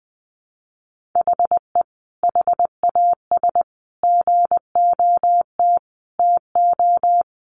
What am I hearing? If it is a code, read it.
Morse code, heard as a keyed tone, about 20 wpm, 700 Hz. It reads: HE HAS GOT TO
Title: Sentences from Top 800 Words 20wpm